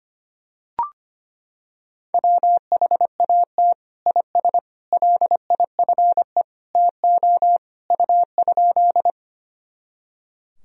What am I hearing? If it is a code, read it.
Morse code, heard as a keyed tone, about 25 wpm, 700 Hz. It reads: WHAT IS LIFE TO U?
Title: Sentences from Top 300 Words and Abbreviations 25wpm